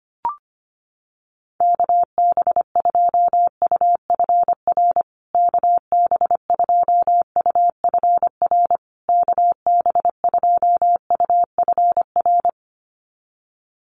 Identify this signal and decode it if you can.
Morse code, heard as a keyed tone, about 25 wpm, 700 Hz. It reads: KB2UFR KB2UFR KB2UFR
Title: USA Call Signs Repeated 3x 25wpm